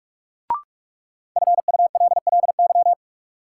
Morse code, heard as a keyed tone, about 46 wpm, 700 Hz. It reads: FULLY
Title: Sentences from Top 1300 Words - New Word Drill - 45wpm